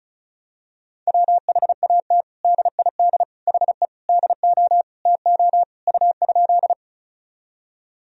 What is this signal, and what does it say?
Morse code, heard as a keyed tone, about 34 wpm, 700 Hz. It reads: WHAT DID HE DO TO U?
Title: Sentences from Top 100 Words and Abbreviations 35wpm